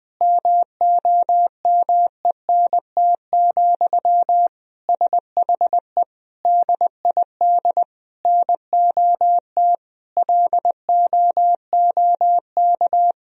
Morse code, heard as a keyed tone, about 20 wpm, 700 Hz. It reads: MOMENT, SHE DID NOT LOOK
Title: Sentences from Top 1000 Words - Review Part 2 - 20wpm